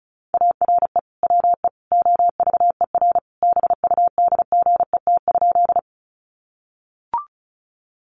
Morse code, heard as a keyed tone, about 34 wpm, 700 Hz. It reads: ARE WE OVER BUDGET?
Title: Sentences from Top 1200 Words 35wpm